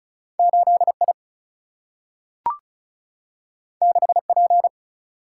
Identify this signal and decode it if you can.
Morse code, heard as a keyed tone, about 36 wpm, 700 Hz. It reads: 8 I 6 P
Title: Single-Letter Single-Number 35wpm